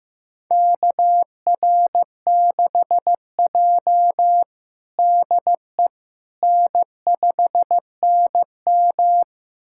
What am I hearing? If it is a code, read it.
Morse code, heard as a keyed tone, about 16 wpm, 700 Hz. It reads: KR6J DE N5NM
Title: USA Callsign DE Callsign - 4 Characters - 15wpm